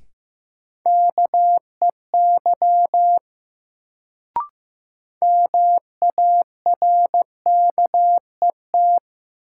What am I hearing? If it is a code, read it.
Morse code, heard as a keyed tone, about 15 wpm, 700 Hz. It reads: KEY MARKET
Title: Sentences from Top 600 Words - New Word Drill - 15wpm